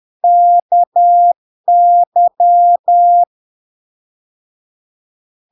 Morse code, heard as a keyed tone, about 10 wpm, 700 Hz. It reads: KY
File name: LARK-FD-2022-Log-in-CW-10wpm-700-Hz